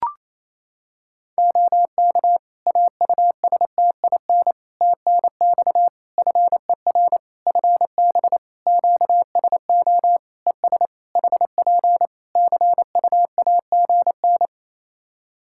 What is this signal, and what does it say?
Morse code, heard as a keyed tone, about 28 wpm, 700 Hz. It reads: OK AUSTIN TNX FER FB QSO ES HP CUAGN
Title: Practice QSO - Exchange 4 - Ending - 28wpm